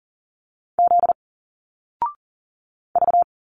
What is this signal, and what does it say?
Morse code, heard as a keyed tone, about 40 wpm, 700 Hz. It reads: Z V
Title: Single Letters 40wpm